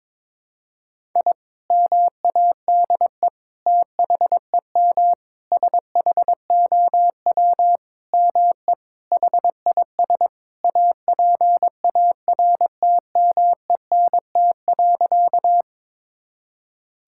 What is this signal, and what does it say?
Morse code, heard as a keyed tone, about 22 wpm, 700 Hz. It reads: I MADE THEM SHOW ME HIS APARTMENT.
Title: Sentences from Top 1000 Words - Review Part 6 - 22wpm